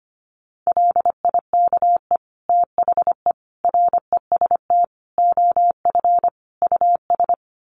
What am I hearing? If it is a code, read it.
Morse code, heard as a keyed tone, about 25 wpm, 700 Hz. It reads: LIKE THE REST OF US
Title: Sets of 5 Words Encore2 25wpm